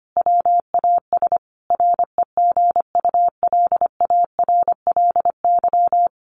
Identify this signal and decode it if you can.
Morse code, heard as a keyed tone, about 25 wpm, 700 Hz. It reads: WAS REGULARLY
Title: Sentences from Top 1400 Words 25wpm